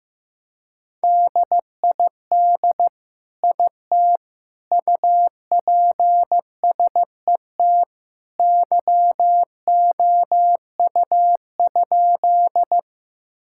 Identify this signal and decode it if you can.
Morse code, heard as a keyed tone, about 15 wpm, 690 Hz. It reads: DID IT UPSET YOU?
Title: Sentences from Top 1000 Words - Review Part 5 - 15wpm